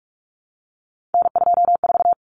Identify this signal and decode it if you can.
Morse code, heard as a keyed tone, about 44 wpm, 700 Hz. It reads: N24
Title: Common Cut Numbers - 3-Digit 45wpm